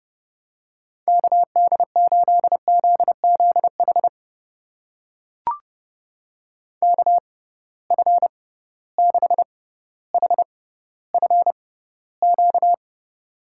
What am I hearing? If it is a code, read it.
Morse code, heard as a keyed tone, about 30 wpm, spaced as for 15 wpm, 700 Hz. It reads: KD8ZZH KF6HFQ
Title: USA Call Signs - 6 Characters - 6x Character Spacing - 30wpm